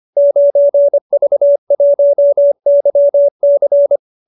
Morse code, heard as a keyed tone, about 25 wpm, 590 Hz. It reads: 9V1YC
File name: qso207b 25